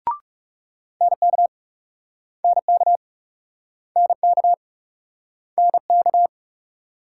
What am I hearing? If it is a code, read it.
Morse code, heard as a keyed tone, about 37 wpm, 700 Hz. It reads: NK NK NK NK
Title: Speed-Racing Multi-letter Phonemes - 30wpm